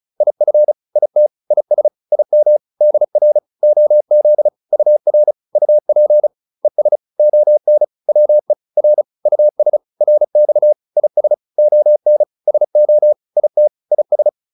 Morse code, heard as a keyed tone, about 35 wpm, 600 Hz. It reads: IF IT IS IM DR OZ UR UP ES ON WE R US RX IS ON SO IT IS